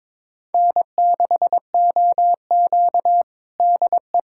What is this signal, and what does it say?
Morse code, heard as a keyed tone, about 22 wpm, 700 Hz. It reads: N6OQ DE
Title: USA Callsign DE Callsign - 4 Characters - 22wpm